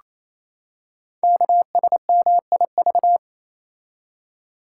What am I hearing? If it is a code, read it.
Morse code, heard as a keyed tone, about 27 wpm, 700 Hz. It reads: KSMIV
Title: Code Groups - 5 Letters - 28wpm